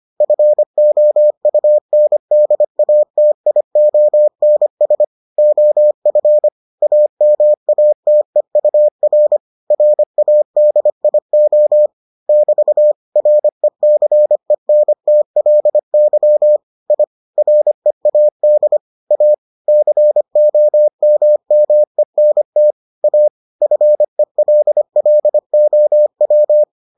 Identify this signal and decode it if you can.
Morse code, heard as a keyed tone, about 25 wpm, 600 Hz. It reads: FOUNDATIONS OF AMATEUR RADIO <BT> RECENTLY I READ A COMMENT A FELLOW